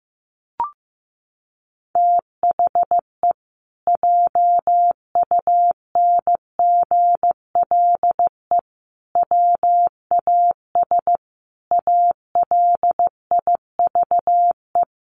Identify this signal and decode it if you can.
Morse code, heard as a keyed tone, about 15 wpm, 700 Hz. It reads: THE JUNGLE WAS ALIVE
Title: A Sound Of Thunder 15wpm